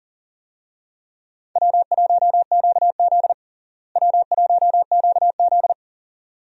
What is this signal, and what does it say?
Morse code, heard as a keyed tone, about 40 wpm, 700 Hz. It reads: W1QZ W1QZ
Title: USA Call Signs - 4 Characters - Repeated 2x - 40wpm